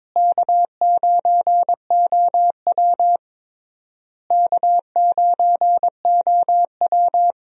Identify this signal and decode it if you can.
Morse code, heard as a keyed tone, about 22 wpm, 700 Hz. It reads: K9OW K9OW
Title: USA Call Signs - 4 Characters - Repeated 3x - 22wpm